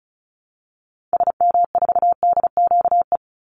Morse code, heard as a keyed tone, about 34 wpm, 700 Hz. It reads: SM4DQE
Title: Call Signs Repeated 3x - 35wpm